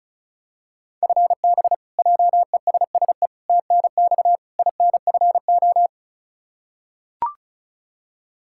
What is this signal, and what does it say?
Morse code, heard as a keyed tone, about 36 wpm, 700 Hz. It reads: FB JESSE TNX INFO
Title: Practice QSO - Exchange 3 - More Info - 35wpm